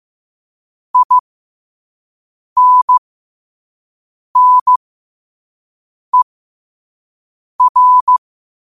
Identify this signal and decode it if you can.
Morse code, heard as a keyed tone, about 15 wpm, spaced as for 5 wpm, 1000 Hz. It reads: INNER
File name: qotdfiles / 5wpm